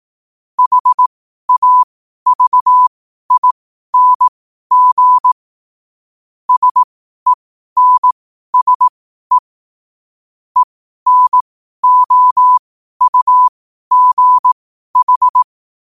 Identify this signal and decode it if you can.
Morse code, heard as a keyed tone, about 18 wpm, spaced as for 12 wpm, 1000 Hz. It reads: HAVING SENSE ENOUGH
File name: qotdfiles / 12wpm